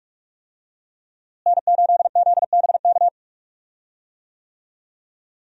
Morse code, heard as a keyed tone, about 42 wpm, 700 Hz. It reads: N8ZBK
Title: USA Call Signs - 5 Characters - Repeated 3x - 45wpm